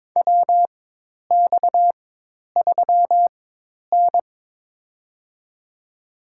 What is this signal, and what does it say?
Morse code, heard as a keyed tone, about 22 wpm, spaced as for 10 wpm, 700 Hz. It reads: WX3N
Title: USA Call Signs - 4x Character Spacing - 22wpm